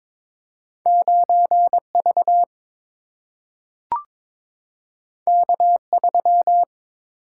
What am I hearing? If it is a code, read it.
Morse code, heard as a keyed tone, about 22 wpm, 700 Hz. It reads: 9V K3
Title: Single-Letter Single-Number 22wpm